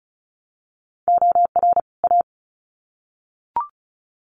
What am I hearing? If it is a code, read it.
Morse code, heard as a keyed tone, about 34 wpm, 700 Hz. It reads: OR A
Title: Sets of 2 Words 35wpm